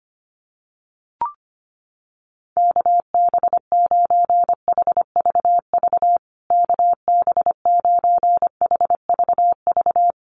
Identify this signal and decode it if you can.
Morse code, heard as a keyed tone, about 25 wpm, 700 Hz. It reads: KB9HVV KB9HVV
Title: USA Call Signs Repeated 3x 25wpm